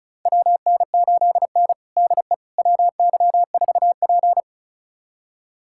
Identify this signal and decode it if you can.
Morse code, heard as a keyed tone, about 36 wpm, 700 Hz. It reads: WN8N DE WY4P
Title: USA Callsign DE Callsign - 4 Characters - 35wpm